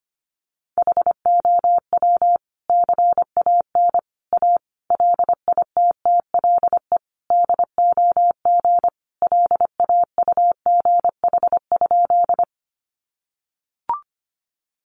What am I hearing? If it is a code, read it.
Morse code, heard as a keyed tone, about 25 wpm, 700 Hz. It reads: HOW CAN A LITTLE DOG LAUGH?
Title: Sentences from Top 500 Words 25wpm